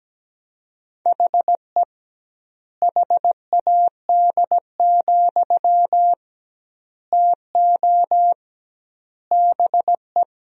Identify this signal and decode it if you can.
Morse code, heard as a keyed tone, about 17 wpm, 700 Hz. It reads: HE HAD, TO BE
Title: 2x Word-Spacing - Sentences from Top 500 Words - 17wpm